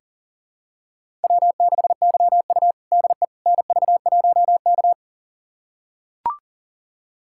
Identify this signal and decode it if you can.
Morse code, heard as a keyed tone, about 40 wpm, 700 Hz. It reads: W6YU DE NV1K
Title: USA Callsign DE Callsign - 4 Characters - 40wpm